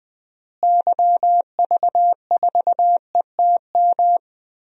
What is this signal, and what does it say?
Morse code, heard as a keyed tone, about 20 wpm, 700 Hz. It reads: YV4ETM
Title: DXCC-100 Call Signs - 6 Characters - Repeated 3x - 20wpm